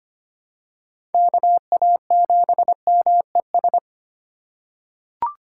Morse code, heard as a keyed tone, about 25 wpm, 700 Hz. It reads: KA7MES
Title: USA Call Signs - 6 Characters - 3x Character Spacing - 25wpm